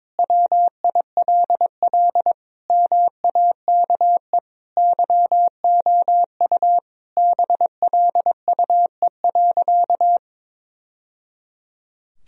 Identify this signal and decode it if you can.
Morse code, heard as a keyed tone, about 22 wpm, 700 Hz. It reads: WILL MAKE YOU BLUE.
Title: Sentences from Top 1000 Words - Review Part 3 - 22wpm